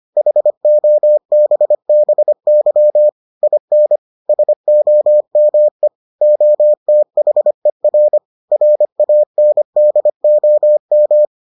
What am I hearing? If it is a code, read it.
Morse code, heard as a keyed tone, about 25 wpm, 600 Hz. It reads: HOBBY IN SOME OTHER RANDOM